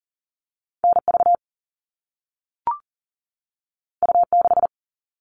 Morse code, heard as a keyed tone, about 40 wpm, 700 Hz. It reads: NV U6
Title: All Cut Numbers - 2-Digit 40wpm